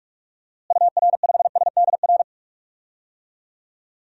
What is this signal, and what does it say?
Morse code, heard as a keyed tone, about 44 wpm, 700 Hz. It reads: UR5SDR
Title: DXCC-100 Call Signs - 6 Characters - Repeated 3x - 45wpm